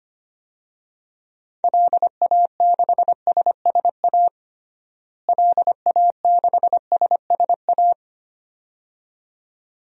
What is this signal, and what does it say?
Morse code, heard as a keyed tone, about 25 wpm, 700 Hz. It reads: LA6SSA LA6SSA
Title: DXCC-100 Call Signs - 6 Characters - Repeated 2x - 25wpm